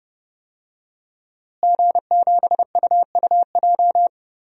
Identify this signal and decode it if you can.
Morse code, heard as a keyed tone, about 30 wpm, 700 Hz. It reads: G7UUJ
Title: DXCC-100 Call Signs - 5 Characters - Repeated 2x - 30wpm